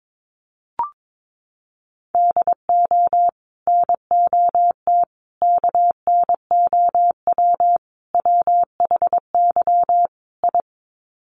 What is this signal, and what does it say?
Morse code, heard as a keyed tone, about 22 wpm, 700 Hz. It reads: DO NOT KNOW WHY I
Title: Sets of 5 Words Encore 22wpm